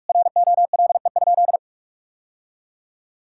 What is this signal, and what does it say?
Morse code, heard as a keyed tone, about 46 wpm, 700 Hz. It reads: ROLE?
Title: Sentences from Top 1000 Words - Review Part 6 - 45wpm